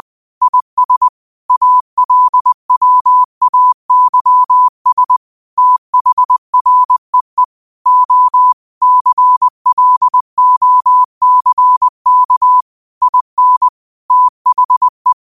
Morse code, heard as a keyed tone, about 20 wpm, 1000 Hz. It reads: IS ALWAYS THREE O CLOCK IN THE